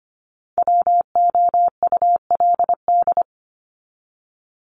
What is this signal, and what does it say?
Morse code, heard as a keyed tone, about 25 wpm, 700 Hz. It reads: WOULD
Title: Top 200 Words 25wpm